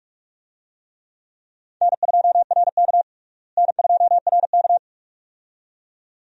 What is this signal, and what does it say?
Morse code, heard as a keyed tone, about 44 wpm, 700 Hz. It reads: N2RK N2RK
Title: USA Call Signs - 4 Characters - Repeated 2x - 45wpm